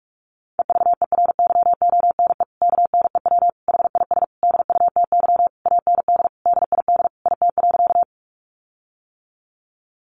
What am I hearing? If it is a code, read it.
Morse code, heard as a keyed tone, about 44 wpm, 700 Hz. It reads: EVERYONE KNEW HIS DUTY AND DID IT.
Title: Sentences from Top 1500 Words 45wpm